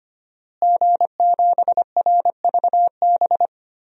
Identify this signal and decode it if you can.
Morse code, heard as a keyed tone, about 25 wpm, 700 Hz. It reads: G7RVB
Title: DXCC-100 Call Signs - 5 Characters - Repeated 2x - 25wpm